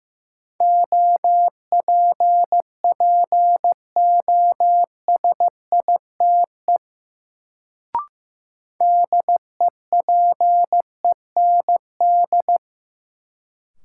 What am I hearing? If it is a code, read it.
Morse code, heard as a keyed tone, about 16 wpm, 700 Hz. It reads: OPPOSITE DEPEND
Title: Sentences from Top 1300 Words - New Word Drill Alt - 15wpm